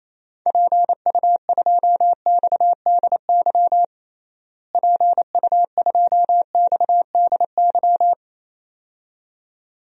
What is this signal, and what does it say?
Morse code, heard as a keyed tone, about 28 wpm, 700 Hz. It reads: PU2XDY PU2XDY
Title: DXCC-100 Call Signs - 6 Characters - Repeated 2x - 28wpm